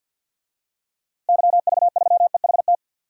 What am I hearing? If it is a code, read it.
Morse code, heard as a keyed tone, about 53 wpm, 700 Hz. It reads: YV3EHT
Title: DXCC-100 Call Signs - 6 Characters - Repeated 3x - 50wpm